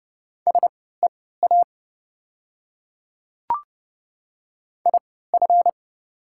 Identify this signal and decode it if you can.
Morse code, heard as a keyed tone, about 30 wpm, spaced as for 17 wpm, 700 Hz. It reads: SEA IF
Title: 3x Character Spacing - Top 300 Words - 30wpm